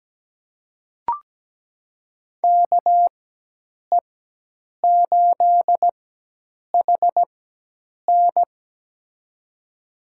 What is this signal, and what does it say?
Morse code, heard as a keyed tone, about 17 wpm, spaced as for 8 wpm, 700 Hz. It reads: KE8HN
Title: USA Call Signs - 4x Character Spacing - 17wpm